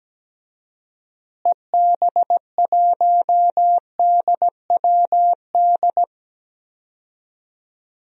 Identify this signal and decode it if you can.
Morse code, heard as a keyed tone, about 17 wpm, 700 Hz. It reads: EB1DWD
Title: DXCC-100 Call Signs - 6 Characters - Repeated 3x - 17wpm